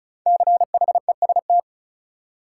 Morse code, heard as a keyed tone, about 35 wpm, 700 Hz. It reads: CHEST
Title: Five Letter Words - 35wpm